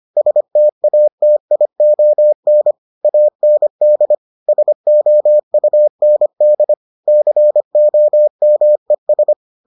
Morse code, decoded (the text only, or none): STATION AND SOUND COMES